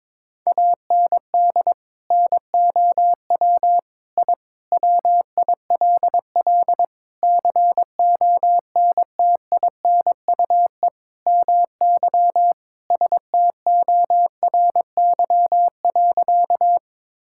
Morse code, heard as a keyed tone, about 22 wpm, 700 Hz. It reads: AND NOW I WILL CONTINUE MY STORY.
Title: Sentences from Top 1000 Words - Review Part 6 - 22wpm